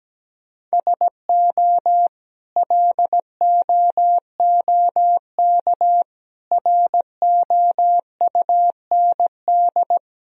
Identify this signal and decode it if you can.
Morse code, heard as a keyed tone, about 17 wpm, 700 Hz. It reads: SO LOOK ROUND
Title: Sentences from Top 1000 Words - Review Part 1 - 17wpm